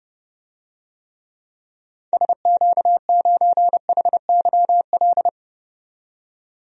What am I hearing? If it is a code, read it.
Morse code, heard as a keyed tone, about 30 wpm, 700 Hz. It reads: SQ9HYL
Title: DXCC-100 Call Signs - 6 Characters - 30wpm